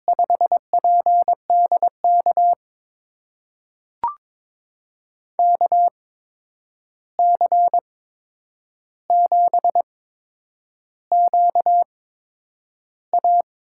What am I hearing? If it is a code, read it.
Morse code, heard as a keyed tone, about 22 wpm, spaced as for 9 wpm, 700 Hz. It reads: 5PDK KC7QA
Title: USA Call Signs - 5 Characters - 8x Character Spacing - 22wpm